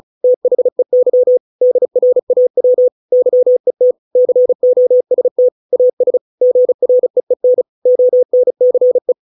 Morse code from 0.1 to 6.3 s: THEY DRAW YET COST AS